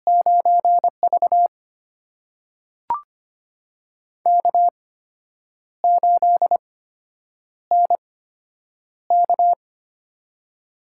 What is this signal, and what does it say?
Morse code, heard as a keyed tone, about 25 wpm, spaced as for 8 wpm, 700 Hz. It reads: 9V K8NK
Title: USA Call Signs - 8x Character Spacing - 25wpm